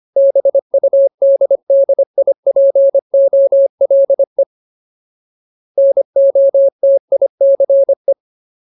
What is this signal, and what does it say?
Morse code, heard as a keyed tone, about 25 wpm, 550 Hz. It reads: BUDDIPOLE NOTICE
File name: Ultimate Word List / 25WPM Ultimate